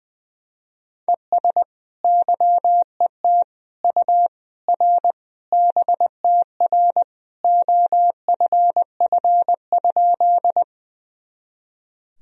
Morse code, heard as a keyed tone, about 20 wpm, 700 Hz. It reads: ES YET U R BTR OFF?
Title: Sentences from Top 500 Words and Abbreviations 20wpm